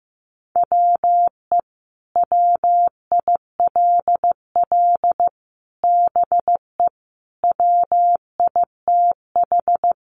Morse code, heard as a keyed tone, about 15 wpm, 700 Hz. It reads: WE WILL BE WITH
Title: Sentences from Top 600 Words 15wpm